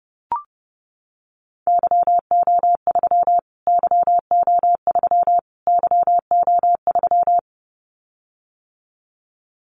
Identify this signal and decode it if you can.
Morse code, heard as a keyed tone, about 30 wpm, 700 Hz. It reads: YO3 YO3 YO3
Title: Callsign Prefixes for Top 25 DXCC - Repeated 3x - 30wpm